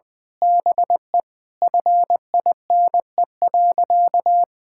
Morse code, heard as a keyed tone, about 20 wpm, 700 Hz. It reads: BE FINE.
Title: Sentences with 7-Word Phrases - 20wpm